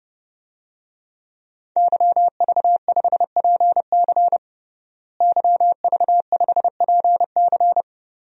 Morse code, heard as a keyed tone, about 30 wpm, 700 Hz. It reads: YV5PC YV5PC
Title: DXCC-100 Call Signs - 5 Characters - Repeated 3x - 30wpm